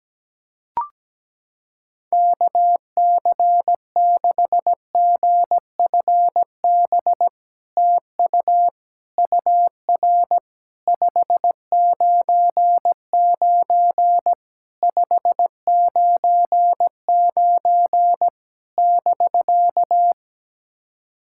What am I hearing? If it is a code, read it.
Morse code, heard as a keyed tone, about 17 wpm, 700 Hz. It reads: KC6GFB TU UR 599 599 <BK>
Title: SOTA QSO Practice 17wpm